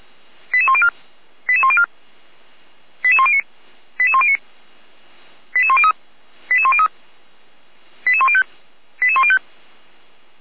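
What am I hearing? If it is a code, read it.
Five-tone selective call (ZVEI), heard as quick runs of five tones, tones of about 70 ms. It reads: 88104 88104 88109 88109 88103 88103 88105 88105
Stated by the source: ZVEI five-tone call received on SDR, fire brigade Feuerwehr Orlamünde.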